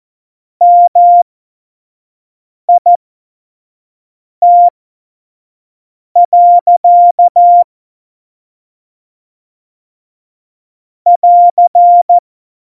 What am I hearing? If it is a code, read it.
Morse code, heard as a keyed tone, about 14 wpm, spaced as for 5 wpm, 700 Hz. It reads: MIT. <AR>
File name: Morsekurs / Witze / Witz 03. November 2020, 25 F70